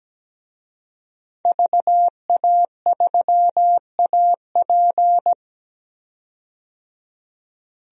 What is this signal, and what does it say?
Morse code, heard as a keyed tone, about 17 wpm, 700 Hz. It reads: VA3AP
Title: DXCC-100 Call Signs - 5 Characters - 17wpm